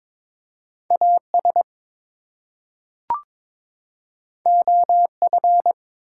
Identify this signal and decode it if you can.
Morse code, heard as a keyed tone, about 23 wpm, 700 Hz. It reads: AS OF
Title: Two Letter Words - 22wpm